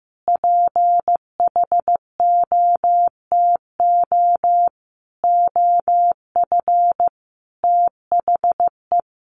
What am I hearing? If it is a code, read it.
Morse code, heard as a keyed tone, about 15 wpm, 700 Hz. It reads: PHOTO OF THE
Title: Sentences from Top 1200 Words 15wpm